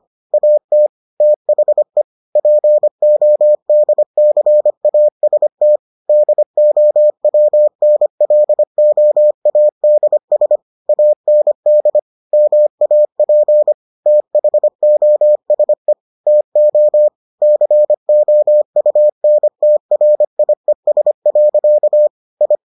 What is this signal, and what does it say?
Morse code, heard as a keyed tone, about 25 wpm, 600 Hz. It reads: AT THE PODCAST DOWNLOADS AND MAP THOSE TO COUNTRIES. I